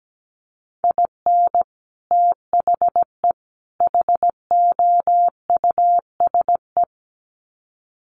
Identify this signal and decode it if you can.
Morse code, heard as a keyed tone, about 17 wpm, 700 Hz. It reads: IN THE HOUSE
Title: Sets of 3 Words 17wpm